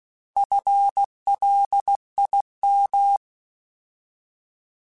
Morse code, heard as a keyed tone, about 16 wpm, 800 Hz. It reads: FLIM
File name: Alphabetical / Alphabetical mp3s / Fs @15